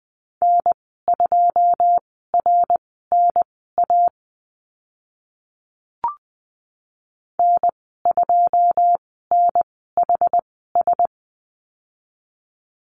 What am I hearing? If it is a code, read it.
Morse code, heard as a keyed tone, about 20 wpm, spaced as for 14 wpm, 700 Hz. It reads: N2RNA N2NHS
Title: 082 Lesson 14g - Number 2 - Call Signs 20wpm